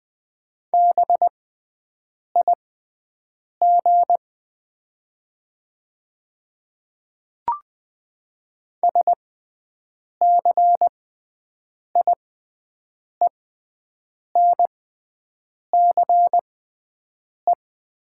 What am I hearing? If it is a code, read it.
Morse code, heard as a keyed tone, about 20 wpm, spaced as for 7 wpm, 690 Hz. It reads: BIG SCIENCE
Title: 6x Character Spacing - Top 500 Words - 20wpm